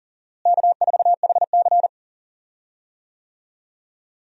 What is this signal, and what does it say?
Morse code, heard as a keyed tone, about 40 wpm, 700 Hz. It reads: K4HC
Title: USA Call Signs - 4 Characters - Repeated 2x - 40wpm